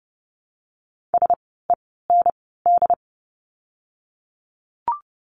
Morse code, heard as a keyed tone, about 30 wpm, 700 Hz. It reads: S E N D
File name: Lesson 16f - Letter c - Words 30wpm